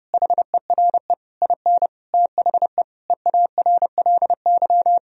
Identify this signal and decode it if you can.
Morse code, heard as a keyed tone, about 32 wpm, 700 Hz. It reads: HERE IN THE EARLY
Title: Sentences from Top 1000 Words - Review Part 3 - 30wpm